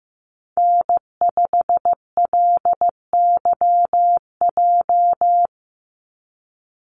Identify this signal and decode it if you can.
Morse code, heard as a keyed tone, about 15 wpm, 700 Hz. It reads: N5LYJ
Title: Call Signs Repeated 2x - 15wpm